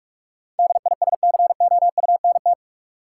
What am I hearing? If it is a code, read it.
Morse code, heard as a keyed tone, about 46 wpm, 700 Hz. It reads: DISCOUNT
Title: Sentences from Top 1200 Words - New Word Drill - 45wpm